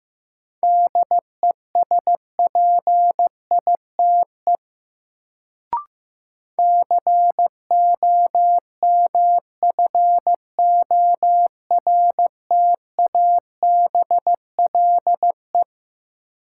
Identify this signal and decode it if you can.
Morse code, heard as a keyed tone, about 15 wpm, 700 Hz. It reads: DESPITE COMFORTABLE
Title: Sentences from Top 1000 Words - Review Part 7 - New Word Drill - 15wpm